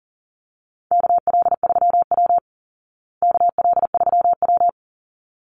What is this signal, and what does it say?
Morse code, heard as a keyed tone, about 40 wpm, 700 Hz. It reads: KL3W KL3W
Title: Call Signs Repeated 3x - 40wpm